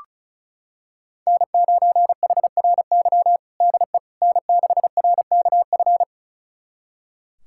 Morse code, heard as a keyed tone, about 36 wpm, 700 Hz. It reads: N9HRY DE N6RKF
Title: USA Callsign DE Callsign - 5 Characters - 35wpm